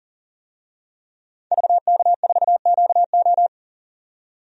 Morse code, heard as a keyed tone, about 40 wpm, 700 Hz. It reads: VK4QO
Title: DXCC-100 Call Signs - 5 Characters - 40wpm